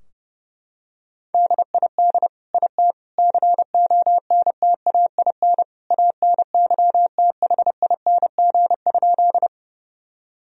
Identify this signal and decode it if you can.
Morse code, heard as a keyed tone, about 30 wpm, 700 Hz. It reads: DID IT CONTAIN ANYTHING?
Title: Sentences from Top 1000 Words - Review Part 2 - 30wpm